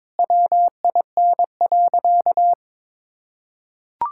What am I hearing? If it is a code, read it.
Morse code, heard as a keyed tone, about 22 wpm, 700 Hz. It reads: WIN.
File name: practice / 22wpm / Sentences with 3-Word Phrases - 22wpm